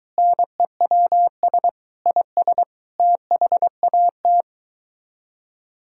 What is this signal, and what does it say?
Morse code, heard as a keyed tone, about 23 wpm, 700 Hz. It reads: NEWS IS THAT